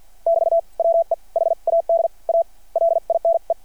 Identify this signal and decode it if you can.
Morse code, heard as a keyed tone, about 49 wpm, spaced as for 42 wpm, 660 Hz. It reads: <BT> WE HAD A LINE
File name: archived / CRSnet